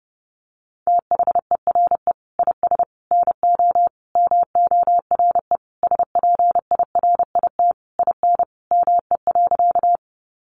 Morse code, heard as a keyed tone, about 28 wpm, 700 Hz. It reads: THERE IS NO MORE SPIRIT IN ME.
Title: Sentences from Top 1400 Words 30wpm